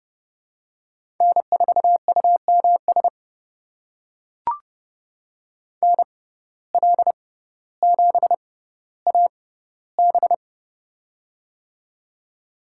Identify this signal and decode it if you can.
Morse code, heard as a keyed tone, about 30 wpm, spaced as for 15 wpm, 700 Hz. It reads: N4UMS NL7AB
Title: USA Call Signs - 5 Characters - 6x Character Spacing - 30wpm